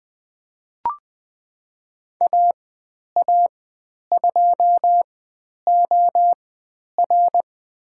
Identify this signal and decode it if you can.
Morse code, heard as a keyed tone, about 20 wpm, spaced as for 10 wpm, 700 Hz. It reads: AA2OR
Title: 095 Lesson 16g - Letter c - Call Signs 20wpm